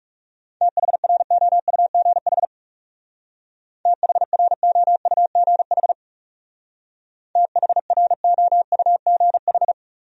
Morse code, heard as a keyed tone, about 35 wpm, 700 Hz. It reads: THROUGH THROUGH THROUGH
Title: Speed-Racing Top 300 Words 30wpm